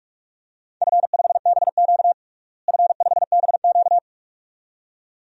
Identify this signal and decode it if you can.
Morse code, heard as a keyed tone, about 46 wpm, 700 Hz. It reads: F5BQ F5BQ
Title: DXCC-100 Call Signs - 4 Characters - Repeated 2x - 45wpm